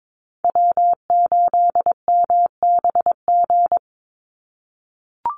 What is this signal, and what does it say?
Morse code, heard as a keyed tone, about 22 wpm, 700 Hz. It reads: W8MBG
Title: USA Call Signs 22wpm